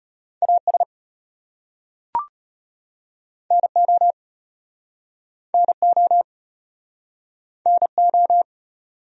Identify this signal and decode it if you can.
Morse code, heard as a keyed tone, about 36 wpm, 700 Hz. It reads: AS NO NO NO